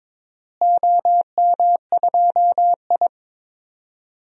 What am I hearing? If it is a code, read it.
Morse code, heard as a keyed tone, about 22 wpm, 700 Hz. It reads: OM2I
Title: DXCC-100 Call Signs - 4 Characters - Repeated 3x - 22wpm